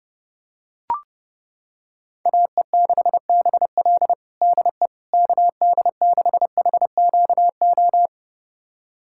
Morse code, heard as a keyed tone, about 30 wpm, 700 Hz. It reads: AE6BL DE KD6HQO
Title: USA Callsign DE Callsign 30wpm